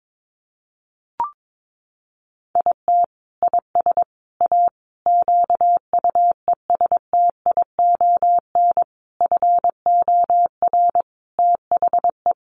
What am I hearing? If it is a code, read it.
Morse code, heard as a keyed tone, about 22 wpm, 700 Hz. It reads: IT IS A QUESTION FOR THE
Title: Sentences from Top 1300 Words 22wpm